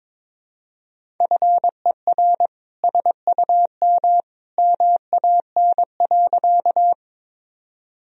Morse code, heard as a keyed tone, about 22 wpm, 700 Hz. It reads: FER SUM MAN.
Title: Sentences from Top 200 Words and Abbreviations 22wpm